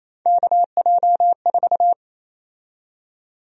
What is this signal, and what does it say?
Morse code, heard as a keyed tone, about 28 wpm, 700 Hz. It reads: KJ4
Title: USA Call Sign Prefixes - 28wpm